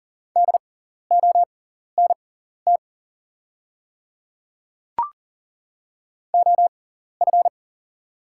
Morse code, heard as a keyed tone, about 37 wpm, spaced as for 13 wpm, 690 Hz. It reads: DONT OF
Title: 6x Character Spacing - Top 500 Words - 40wpm